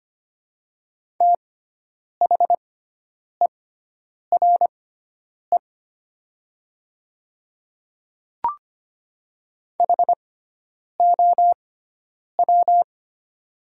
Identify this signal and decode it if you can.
Morse code, heard as a keyed tone, about 25 wpm, spaced as for 9 wpm, 700 Hz. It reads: THERE HOW
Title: 6x Character Spacing - Top 100 Words - 25wpm